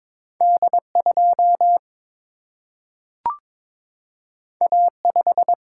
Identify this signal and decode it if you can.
Morse code, heard as a keyed tone, about 22 wpm, 700 Hz. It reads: D 2 A 5
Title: Single-Letter Single-Number 22wpm